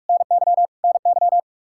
Morse code, heard as a keyed tone, about 41 wpm, 700 Hz. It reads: NY NY
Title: Practice QSO - Exchange 1 - Beginning - 45wpm